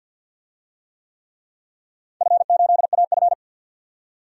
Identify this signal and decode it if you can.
Morse code, heard as a keyed tone, about 49 wpm, 700 Hz. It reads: F8AF
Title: DXCC-100 Call Signs - 4 Characters - 50wpm